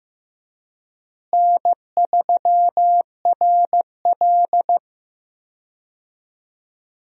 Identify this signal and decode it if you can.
Morse code, heard as a keyed tone, about 15 wpm, 700 Hz. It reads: N3RL
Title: USA Call Signs - 4 Characters - Repeated 3x - 15wpm